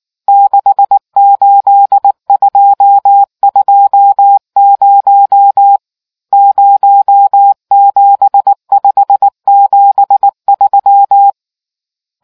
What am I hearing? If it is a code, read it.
Morse code, heard as a keyed tone, about 19 wpm, 800 Hz. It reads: 68220 07573